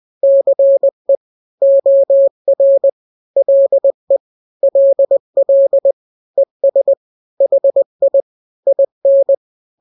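Morse code, heard as a keyed tone, about 20 wpm, 550 Hz. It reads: CE OR LE LL ES HI IN
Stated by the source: Morse drill of two-letter n-grams at 20/18 WPM (Farnsworth), from KK7RR.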